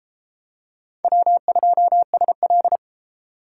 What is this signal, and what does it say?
Morse code, heard as a keyed tone, about 32 wpm, 700 Hz. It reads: W2SL